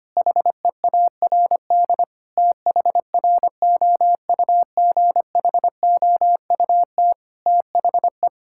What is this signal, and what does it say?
Morse code, heard as a keyed tone, about 25 wpm, 700 Hz. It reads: HEARD THROUGHOUT THE
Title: Sentences from Top 1000 Words - Review Part 5 - 25wpm